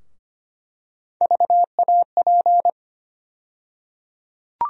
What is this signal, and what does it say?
Morse code, heard as a keyed tone, about 25 wpm, 700 Hz. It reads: VAP
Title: USA Call Sign Postfixes - 3 Letters 25wpm